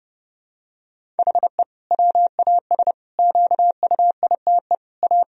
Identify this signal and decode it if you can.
Morse code, heard as a keyed tone, about 30 wpm, 700 Hz. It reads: HE WAS QUITE A
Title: Sentences from Top 1000 Words - Review Part 3 - 30wpm